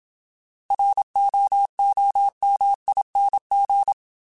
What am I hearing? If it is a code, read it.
Morse code, heard as a keyed tone, about 27 wpm, 800 Hz. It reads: ROOMING
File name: Alphabetical / Alphabetical mp3s / Rs @25